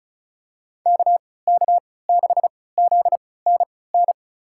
Morse code, heard as a keyed tone, about 36 wpm, spaced as for 20 wpm, 700 Hz. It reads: KK6ZNN
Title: USA Call Signs - 3x Character Spacing - 35wpm